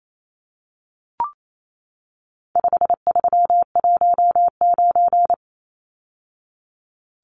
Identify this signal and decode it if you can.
Morse code, heard as a keyed tone, about 28 wpm, 700 Hz. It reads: 5319
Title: Numbers - 4-Digit 28wpm